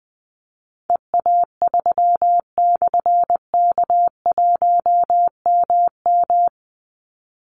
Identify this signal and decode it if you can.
Morse code, heard as a keyed tone, about 19 wpm, 700 Hz. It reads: EA3/K1MM
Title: Call Signs - 20wpm